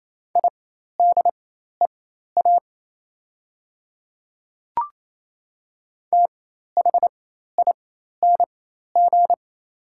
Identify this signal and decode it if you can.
Morse code, heard as a keyed tone, about 27 wpm, spaced as for 13 wpm, 700 Hz. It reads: IDEA THING
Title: 4x Character Spacing - Top 300 Words - 28wpm